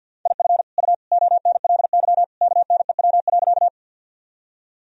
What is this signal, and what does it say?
Morse code, heard as a keyed tone, about 50 wpm, 700 Hz. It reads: IF U ONLY KNEW.
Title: Sentences from Top 400 Words and Abbreviations 50wpm